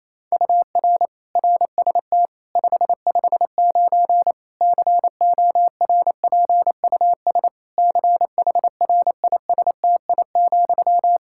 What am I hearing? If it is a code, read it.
Morse code, heard as a keyed tone, about 28 wpm, 700 Hz. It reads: UR RST 559 CORPUS CHRISTI,
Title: Practice QSO - Exchange 1 - Beginning - 28wpm